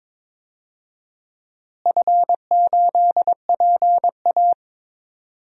Morse code, heard as a keyed tone, about 22 wpm, 700 Hz. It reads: F8PA
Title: DXCC-100 Call Signs - 4 Characters - Repeated 2x - 22wpm